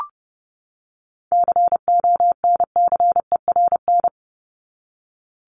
Morse code, heard as a keyed tone, about 30 wpm, 700 Hz. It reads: CONCERN
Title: Top 751-1000 Words 30wpm